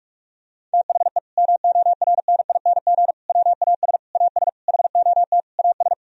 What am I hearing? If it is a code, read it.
Morse code, heard as a keyed tone, about 44 wpm, 700 Hz. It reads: THE MORNING WAS AS HOT AS
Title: Sentences from Top 1000 Words - Review Part 3 - 45wpm